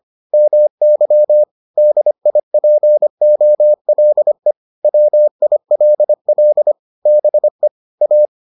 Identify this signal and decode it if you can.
Morse code, heard as a keyed tone, about 25 wpm, 600 Hz. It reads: MY DIPOLE WILL BE A